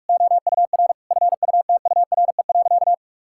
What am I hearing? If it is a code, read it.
Morse code, heard as a keyed tone, about 41 wpm, 700 Hz. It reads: OUR FUTURE.
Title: Sentences from Top 1000 Words - Review Part 6 - 45wpm